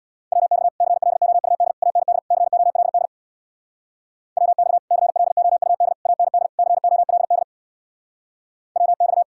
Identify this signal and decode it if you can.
Morse code, heard as a keyed tone, about 62 wpm, 700 Hz. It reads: WX CLOUD AND COLD WX CLOUD AND COLD WX
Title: Speed-Racing Common QSO Phrases Encore - 50wpm